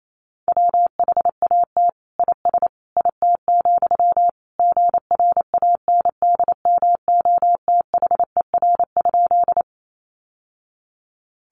Morse code, heard as a keyed tone, about 28 wpm, 700 Hz. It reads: WHAT IS IT, GRANDMOTHER?
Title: Sentences from Top 1200 Words 28wpm